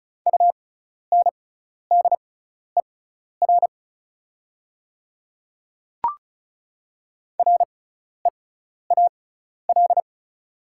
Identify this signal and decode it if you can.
Morse code, heard as a keyed tone, about 36 wpm, spaced as for 12 wpm, 700 Hz. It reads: UNDER REAL
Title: 6x Character Spacing - Top 500 Words - 35wpm